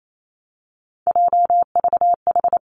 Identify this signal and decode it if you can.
Morse code, heard as a keyed tone, about 28 wpm, 700 Hz. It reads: JVH
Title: Three Challenging Letter-Number Combinations 28wpm